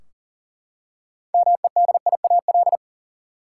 Morse code, heard as a keyed tone, about 40 wpm, 700 Hz. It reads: MEDIAL
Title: K1USN SST QSO Practice 40wpm